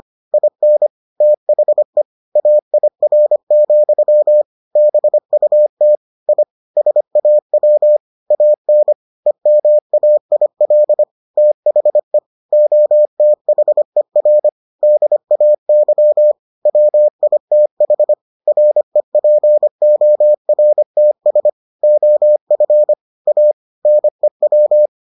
Morse code, heard as a keyed tone, about 25 wpm, 600 Hz. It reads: IN THE AIR, BUT I SAW AN EMAIL THE OTHER DAY WITH REPORTS OF A NEW